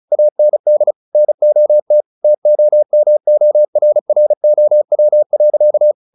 Morse code, heard as a keyed tone, about 35 wpm, 600 Hz. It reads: AND NOT TOMORROW.
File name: the old man and the sea 2 35wpm